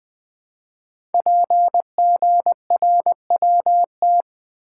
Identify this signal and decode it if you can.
Morse code, heard as a keyed tone, about 20 wpm, 700 Hz. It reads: PGRWT
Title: Code Groups - 5 Letters - 20wpm